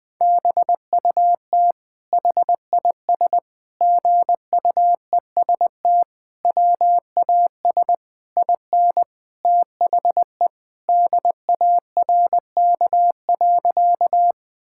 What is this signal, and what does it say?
Morse code, heard as a keyed tone, about 20 wpm, 700 Hz. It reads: BUT HIS GUEST WAS IN THE DARK.
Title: Sentences from Top 1600 Words 20wpm